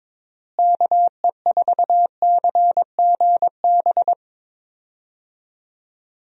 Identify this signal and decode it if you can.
Morse code, heard as a keyed tone, about 22 wpm, 700 Hz. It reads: KE4CGB
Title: USA Call Signs - 6 Characters - Repeated 3x - 22wpm